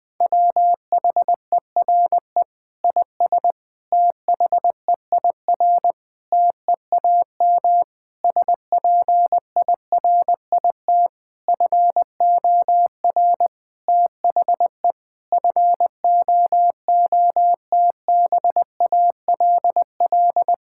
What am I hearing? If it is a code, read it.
Morse code, heard as a keyed tone, about 20 wpm, 700 Hz. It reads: WHERE IS THEIR TEAM SPIRIT FOR THE FOOTBALL
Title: Sentences from Top 1600 Words 20wpm